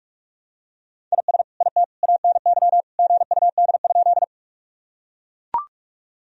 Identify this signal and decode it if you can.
Morse code, heard as a keyed tone, about 44 wpm, 700 Hz. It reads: IS IT ANY GUD?
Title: Sentences from Top 200 Words and Abbreviations 45wpm